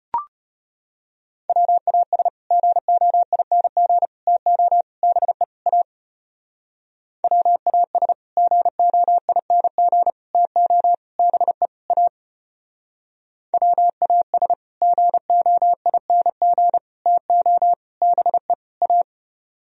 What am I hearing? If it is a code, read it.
Morse code, heard as a keyed tone, about 32 wpm, 700 Hz. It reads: WAS GOING TO BE A WAS GOING TO BE A WAS GOING TO BE A